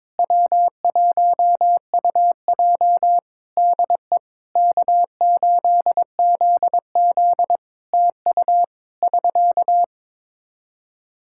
Morse code, heard as a keyed tone, about 22 wpm, 700 Hz. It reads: W1UJ DE K8ZZ TU <SK>
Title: Practice QSO - Exchange 4 - Ending - 22wpm